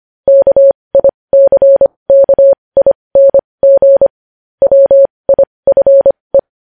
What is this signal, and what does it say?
Morse code, heard as a keyed tone, about 25 wpm, spaced as for 20 wpm, 550 Hz. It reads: KICKING WIFE
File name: podcast / News headlines-1x-25x20